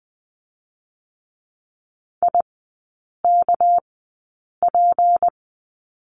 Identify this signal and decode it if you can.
Morse code, heard as a keyed tone, about 20 wpm, spaced as for 8 wpm, 700 Hz. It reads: IKP
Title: 198 Lesson 32e - Letter k - Three Characters 20wpm